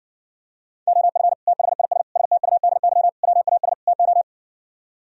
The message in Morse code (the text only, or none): OF THIS STUDY WAS TO